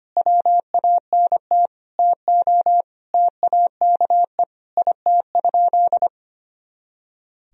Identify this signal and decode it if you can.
Morse code, heard as a keyed tone, about 25 wpm, 700 Hz. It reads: WANT TO TAKE IT?
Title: Sentences from Top 300 Words and Abbreviations 25wpm